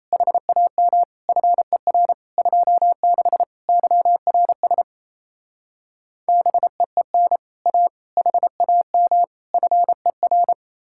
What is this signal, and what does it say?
Morse code, heard as a keyed tone, about 37 wpm, 700 Hz. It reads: HAM FER 26 YRS BEEN A HAM FER